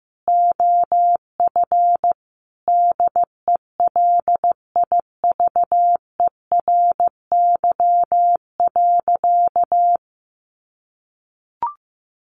Morse code, heard as a keyed tone, about 15 wpm, 700 Hz. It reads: OF DELIVERY.
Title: Sentences from Top 1500 Words 15wpm